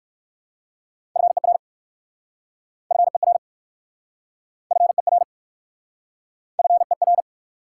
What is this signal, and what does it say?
Morse code, heard as a keyed tone, about 59 wpm, 700 Hz. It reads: FER FER FER FER